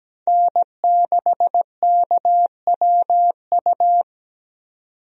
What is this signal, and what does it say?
Morse code, heard as a keyed tone, about 17 wpm, 700 Hz. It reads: N6KWU
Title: USA Call Signs - 5 Characters - 17wpm